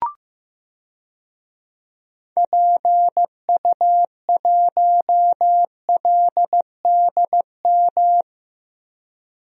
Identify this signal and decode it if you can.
Morse code, heard as a keyed tone, about 15 wpm, 700 Hz. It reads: PU1LDM
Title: DXCC-100 Call Signs - 6 Characters - Repeated 3x - 15wpm